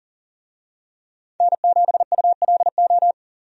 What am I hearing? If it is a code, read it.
Morse code, heard as a keyed tone, about 40 wpm, 700 Hz. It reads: N7ULO
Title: USA Call Signs - 5 Characters - 40wpm